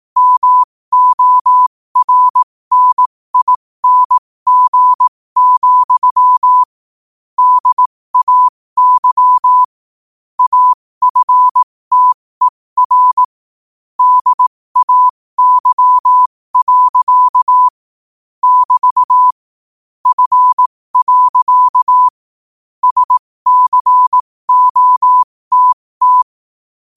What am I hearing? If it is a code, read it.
Morse code, heard as a keyed tone, about 18 wpm, spaced as for 15 wpm, 1000 Hz. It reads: MORNING, DAY AFTER DAY. <BT> F. SCOTT
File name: qotdfiles / 15wpm